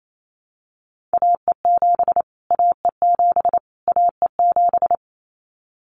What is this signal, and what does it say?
Morse code, heard as a keyed tone, about 28 wpm, 700 Hz. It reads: AE7 AE7 AE7
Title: Callsign Prefixes for Top 25 DXCC - Repeated 3x - 28wpm